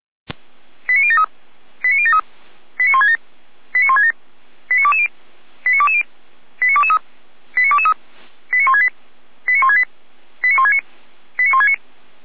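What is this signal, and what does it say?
Five-tone selective call (ZVEI), heard as quick runs of five tones, tones of about 80 ms. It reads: 89073 89073 89167 89167 89220 89220 89223 89223 89168 89168 89169 89169
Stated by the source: ZVEI five-tone selective call received by SDR, fire brigade feuerwehr-orlamuende, address 89022.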